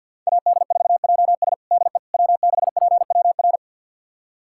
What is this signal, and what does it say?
Morse code, heard as a keyed tone, about 52 wpm, 700 Hz. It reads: AD4JS DE W6PWR
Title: USA Callsign DE Callsign - 5 Characters - 50wpm